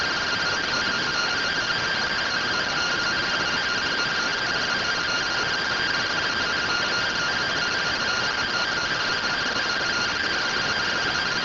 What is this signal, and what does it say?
Two-tone teleprinter signal, about 170 Hz shift.